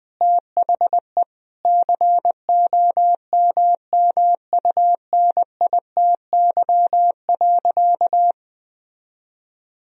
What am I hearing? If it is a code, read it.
Morse code, heard as a keyed tone, about 21 wpm, 700 Hz. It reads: THE COMMUNITY.
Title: Sentences from Top 1000 Words - Review Part 4 - 20wpm